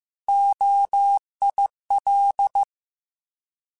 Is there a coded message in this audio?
Yes: Morse code from 0.3 to 2.6 s, heard as a keyed tone, about 15 wpm, 800 Hz. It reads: OIL